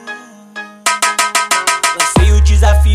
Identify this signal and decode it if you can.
There is no code.